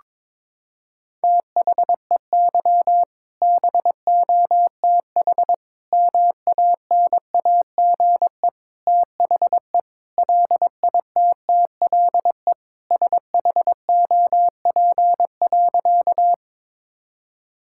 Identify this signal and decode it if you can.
Morse code, heard as a keyed tone, about 22 wpm, 700 Hz. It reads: THEY BOTH MANAGE THE LITTLE SHOP.
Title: Sentences from Top 1600 Words 22wpm